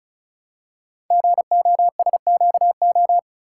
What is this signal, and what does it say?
Morse code, heard as a keyed tone, about 36 wpm, 700 Hz. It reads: GOSQO
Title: Code Groups - 5 Letters - 35wpm